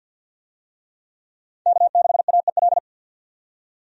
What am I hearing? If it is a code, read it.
Morse code, heard as a keyed tone, about 52 wpm, 700 Hz. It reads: K6AEL
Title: USA Call Signs - 5 Characters - 50wpm